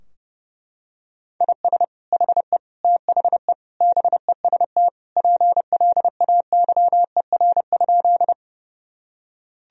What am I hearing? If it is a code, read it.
Morse code, heard as a keyed tone, about 32 wpm, 700 Hz. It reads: IS HE THE BEST PLAYER?
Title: Sentences from Top 1000 Words - Review Part 4 - 30wpm